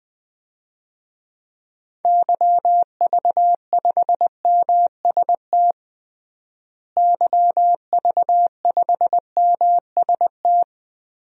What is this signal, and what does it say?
Morse code, heard as a keyed tone, about 20 wpm, 700 Hz. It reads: YV5MST YV5MST
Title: DXCC-100 Call Signs - 6 Characters - Repeated 2x - 20wpm